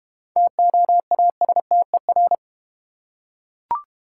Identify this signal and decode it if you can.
Morse code, heard as a keyed tone, about 32 wpm, 700 Hz. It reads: TOASTER